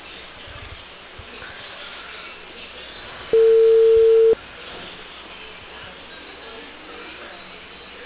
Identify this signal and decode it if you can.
Ringback tone (China, 450 Hz).